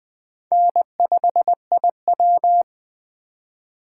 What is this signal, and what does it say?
Morse code, heard as a keyed tone, about 20 wpm, 700 Hz. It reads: N5IW
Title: USA Call Signs - 4 Characters - 2x Character Spacing - 20wpm